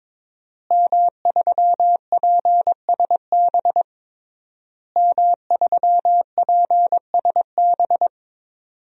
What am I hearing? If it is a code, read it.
Morse code, heard as a keyed tone, about 22 wpm, 700 Hz. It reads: M3PSB M3PSB
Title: DXCC-100 Call Signs - 5 Characters - Repeated 3x - 22wpm